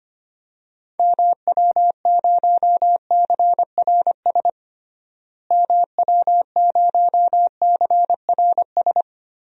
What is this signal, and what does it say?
Morse code, heard as a keyed tone, about 25 wpm, 700 Hz. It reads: MW0CRS MW0CRS
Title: DXCC-100 Call Signs - 6 Characters - Repeated 2x - 25wpm